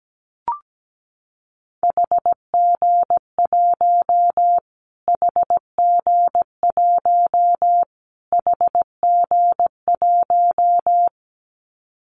Morse code, heard as a keyed tone, about 17 wpm, 700 Hz. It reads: HG1 HG1 HG1
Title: Callsign Prefixes for Top 25 DXCC - Repeated 3x - 17wpm